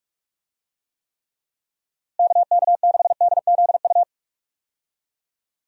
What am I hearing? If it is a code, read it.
Morse code, heard as a keyed tone, about 42 wpm, 700 Hz. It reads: KK6DZU
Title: USA Call Signs - 6 Characters - Repeated 3x - 45wpm